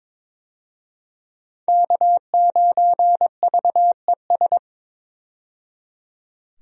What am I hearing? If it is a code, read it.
Morse code, heard as a keyed tone, about 22 wpm, 700 Hz. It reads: K9VES
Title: USA Call Signs - 5 Characters - 22wpm